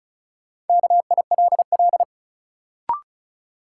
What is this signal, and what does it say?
Morse code, heard as a keyed tone, about 36 wpm, 700 Hz. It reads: KILL
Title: Sentences from Top 1400 Words - New Word Drill Alt - 35wpm